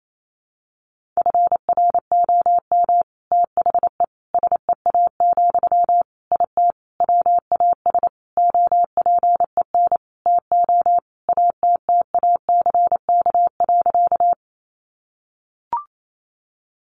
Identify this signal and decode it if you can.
Morse code, heard as a keyed tone, about 28 wpm, 700 Hz. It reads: FROM THE SEA, IT WAS OPEN TO ATTACK.
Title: Sentences from Top 1500 Words 28wpm